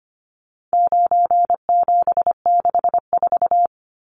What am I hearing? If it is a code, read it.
Morse code, heard as a keyed tone, about 25 wpm, 700 Hz. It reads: 9764
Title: Numbers - 4-Digit 25wpm